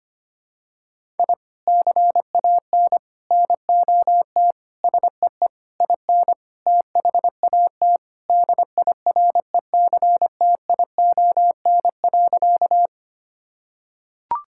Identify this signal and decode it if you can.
Morse code, heard as a keyed tone, about 25 wpm, 700 Hz. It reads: I CAN NOT SEE IN THAT DIRECTION.
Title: Sentences from Top 1000 Words - Review Part 5 - 25wpm